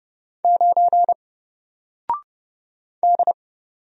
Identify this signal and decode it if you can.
Morse code, heard as a keyed tone, about 30 wpm, 700 Hz. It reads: 9 D
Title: Single Letter-Number Alt 30wpm